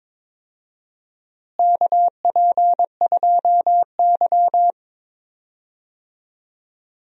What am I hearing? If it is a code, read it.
Morse code, heard as a keyed tone, about 22 wpm, 700 Hz. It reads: KP2Y
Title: DXCC-100 Call Signs - 4 Characters - Repeated 2x - 22wpm